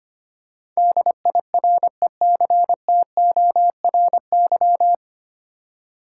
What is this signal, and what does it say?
Morse code, heard as a keyed tone, about 25 wpm, 700 Hz. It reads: DIRECTORY
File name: Nine Letter Words - 25wpm